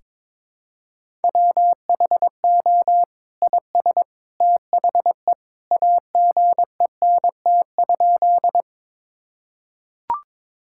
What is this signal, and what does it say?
Morse code, heard as a keyed tone, about 22 wpm, 700 Hz. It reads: WHO IS THE AGENT?
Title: Sentences from Top 1600 Words 22wpm